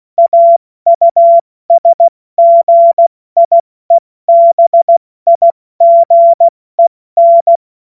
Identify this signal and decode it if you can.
Morse code, heard as a keyed tone, about 16 wpm, 680 Hz. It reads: AUSGIEBIGEN
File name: Witz-16-September-2025-70-F80